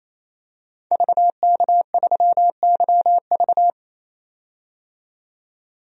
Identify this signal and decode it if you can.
Morse code, heard as a keyed tone, about 27 wpm, 700 Hz. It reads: VK3YV
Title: DXCC-100 Call Signs - 5 Characters - 28wpm